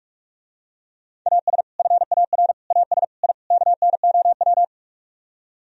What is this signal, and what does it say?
Morse code, heard as a keyed tone, about 49 wpm, 700 Hz. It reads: AS FAR AS I KNOW